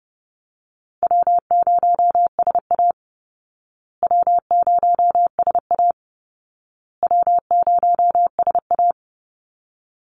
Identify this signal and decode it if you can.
Morse code, heard as a keyed tone, about 30 wpm, 700 Hz. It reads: W0SA W0SA W0SA
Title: Call Signs Repeated 2x - 30wpm